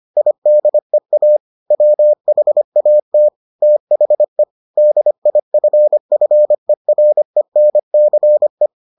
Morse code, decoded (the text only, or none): IDEA WHAT THE DIFFERENCE